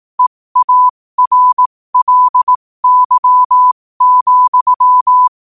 Morse code, heard as a keyed tone, about 18 wpm, spaced as for 15 wpm, 1000 Hz. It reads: EARLY,
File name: qotdfiles / 15wpm